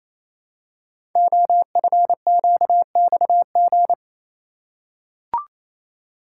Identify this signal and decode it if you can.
Morse code, heard as a keyed tone, about 28 wpm, 700 Hz. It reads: OFQXG
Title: Code Groups - 5 Letters - 28wpm